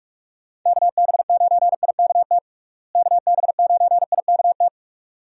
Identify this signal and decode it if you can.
Morse code, heard as a keyed tone, about 46 wpm, 700 Hz. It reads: KB9IKT KB9IKT
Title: USA Call Signs - 6 Characters - Repeated 3x - 45wpm